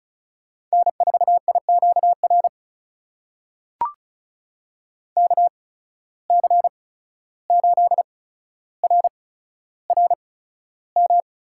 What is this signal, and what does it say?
Morse code, heard as a keyed tone, about 36 wpm, spaced as for 14 wpm, 700 Hz. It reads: N4IQR KC8RRM
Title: USA Call Signs - 8x Character Spacing - 35wpm